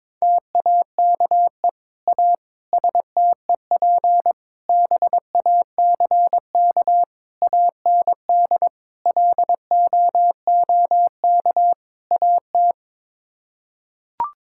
Morse code, heard as a keyed tone, about 21 wpm, 700 Hz. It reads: TAKE A STEP BACK AND LOOK AT
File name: practice / 22wpm / Sets of 7 Words - 22wpm